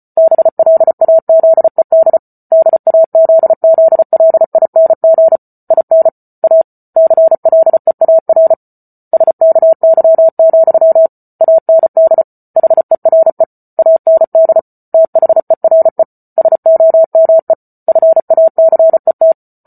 Morse code, heard as a keyed tone, about 34 wpm, 650 Hz. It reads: BLAZED DAZZLING IN A CLEAR SKY, AND HERE AND THERE SOME FACET